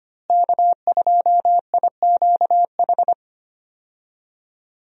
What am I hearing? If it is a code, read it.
Morse code, heard as a keyed tone, about 25 wpm, 700 Hz. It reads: K2IQH
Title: USA Call Signs - 5 Characters - Repeated 3x - 25wpm